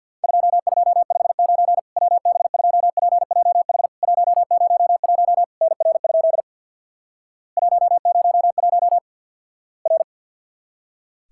Morse code, heard as a keyed tone, about 50 wpm, 700 Hz. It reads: 2259 WB2PJH 101 NR? 101 R
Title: Worked All Europe DX Contest - QTC Exchange with Repeats - 50wpm